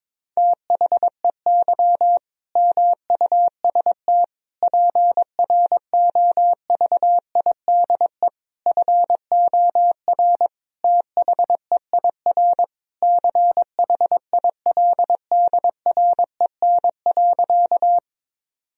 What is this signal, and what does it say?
Morse code, heard as a keyed tone, about 21 wpm, 700 Hz. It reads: THEY MUST PROVIDE FOR THEIR CHILDREN.
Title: Sentences from Top 1000 Words - Review Part 3 - 22wpm